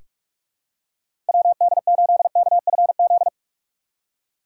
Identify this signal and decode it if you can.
Morse code, heard as a keyed tone, about 46 wpm, 700 Hz. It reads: WD8KFZ
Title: USA Call Signs - 6 Characters - 4x Character Spacing - 45wpm